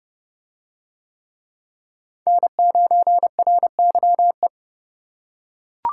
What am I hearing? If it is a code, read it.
Morse code, heard as a keyed tone, about 28 wpm, 700 Hz. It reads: N9RYE